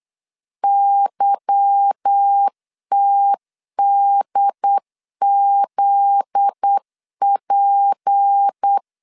Morse code, heard as a keyed tone, about 8 wpm, 800 Hz. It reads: YTDZP
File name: LEZIONI / 8 YJ45 / 8 YJ4 8wpm 3246